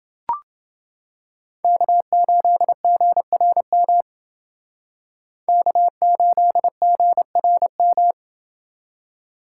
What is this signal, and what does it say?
Morse code, heard as a keyed tone, about 29 wpm, 700 Hz. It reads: K8GRM K8GRM